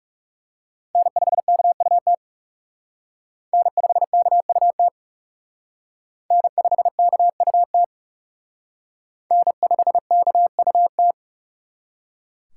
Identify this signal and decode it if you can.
Morse code, heard as a keyed tone, about 36 wpm, 700 Hz. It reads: N5KUT N5KUT N5KUT N5KUT
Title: Speed Racing - USA 1x3 Call Signs 30wpm